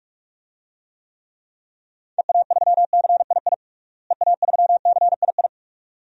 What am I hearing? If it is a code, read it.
Morse code, heard as a keyed tone, about 46 wpm, 700 Hz. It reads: EA3CII EA3CII
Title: DXCC-100 Call Signs - 6 Characters - Repeated 2x - 45wpm